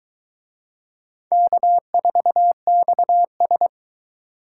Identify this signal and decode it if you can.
Morse code, heard as a keyed tone, about 23 wpm, 700 Hz. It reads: K4XS